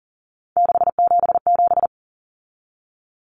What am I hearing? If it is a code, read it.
Morse code, heard as a keyed tone, about 40 wpm, 700 Hz. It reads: 677
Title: Numbers - 3-Digit 40wpm